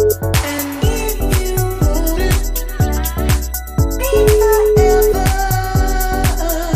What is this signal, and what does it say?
Ringback tone (Europe, 425 Hz), with noise.